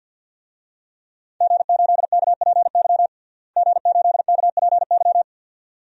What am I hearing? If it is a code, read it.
Morse code, heard as a keyed tone, about 48 wpm, 700 Hz. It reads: G8KPY G8KPY
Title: DXCC-100 Call Signs - 5 Characters - Repeated 2x - 50wpm